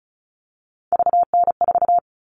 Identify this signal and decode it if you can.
Morse code, heard as a keyed tone, about 34 wpm, 700 Hz. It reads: VN4
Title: All Cut Numbers - 3-Digit 35wpm